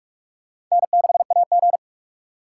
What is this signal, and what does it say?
Morse code, heard as a keyed tone, about 47 wpm, 700 Hz. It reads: N6AG
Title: USA Call Signs - 4 Characters - 6x Character Spacing - 45wpm